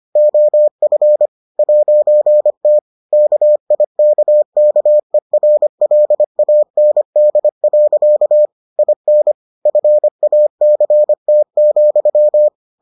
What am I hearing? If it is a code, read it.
Morse code, heard as a keyed tone, about 25 wpm, 600 Hz. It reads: OF 'T KIKKERLAND. IN FACT,